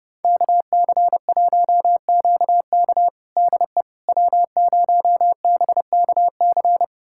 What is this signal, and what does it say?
Morse code, heard as a keyed tone, about 30 wpm, 700 Hz. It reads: KC1QK DE W0BKC
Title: USA Callsign DE Callsign - 5 Characters - 30wpm